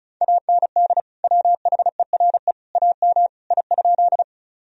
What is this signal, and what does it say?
Morse code, heard as a keyed tone, about 34 wpm, 700 Hz. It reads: AND WHERE AM I?
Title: Sentences from Top 1000 Words - Review Part 2 - 35wpm